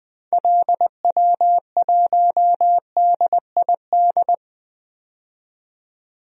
Morse code, heard as a keyed tone, about 20 wpm, 700 Hz. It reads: LW1DID
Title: DXCC-100 Call Signs - 6 Characters - 20wpm